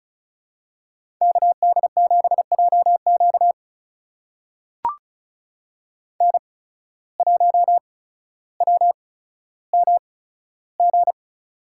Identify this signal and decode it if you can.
Morse code, heard as a keyed tone, about 36 wpm, spaced as for 15 wpm, 700 Hz. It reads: KD7JQ N1WMG
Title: USA Call Signs - 5 Characters - 8x Character Spacing - 35wpm